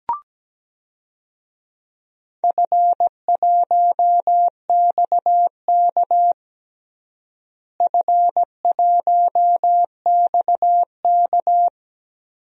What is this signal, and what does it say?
Morse code, heard as a keyed tone, about 17 wpm, 700 Hz. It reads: F1XK F1XK
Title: DXCC-100 Call Signs - 4 Characters - Repeated 2x - 17wpm